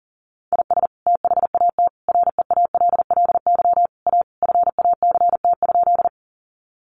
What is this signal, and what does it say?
Morse code, heard as a keyed tone, about 40 wpm, 700 Hz. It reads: IS THAT REALLY A FACT?
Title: Sentences from Top 600 Words 40wpm